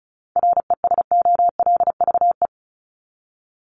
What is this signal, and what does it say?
Morse code, heard as a keyed tone, about 34 wpm, 700 Hz. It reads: RESOLVE
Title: Top 1751-2000 Words 35wpm